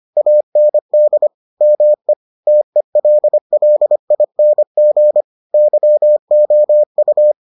Morse code, heard as a keyed tone, about 25 wpm, 600 Hz. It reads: AND ME TELLING YOU